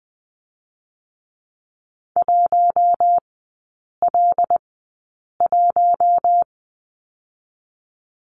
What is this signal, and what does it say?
Morse code, heard as a keyed tone, about 20 wpm, spaced as for 8 wpm, 700 Hz. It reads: 1L1
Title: 166 Lesson 27e - Number 7 - Three Characters 20wpm